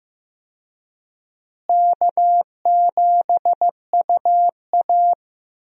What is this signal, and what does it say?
Morse code, heard as a keyed tone, about 15 wpm, 700 Hz. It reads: K7UA
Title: USA Call Signs - 4 Characters - Repeated 2x - 15wpm